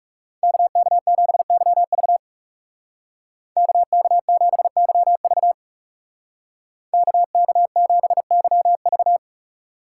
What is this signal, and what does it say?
Morse code, heard as a keyed tone, about 35 wpm, 700 Hz. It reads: KK7YV KK7YV KK7YV
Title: Speed Racing - USA 2x2 Call Signs 30wpm